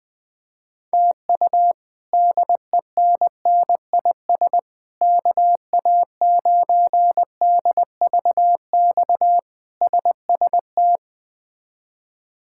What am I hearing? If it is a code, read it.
Morse code, heard as a keyed tone, about 20 wpm, 700 Hz. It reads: TU DENNIS KA9DVX SST
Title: K1USN SST QSO Practice - 20wpm